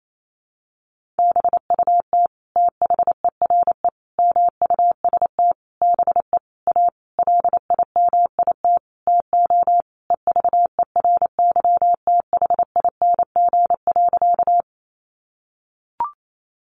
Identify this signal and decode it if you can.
Morse code, heard as a keyed tone, about 28 wpm, 700 Hz. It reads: BUT THERE MUST BE A LIMIT TO EVERYTHING.
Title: Sentences from Top 1500 Words 28wpm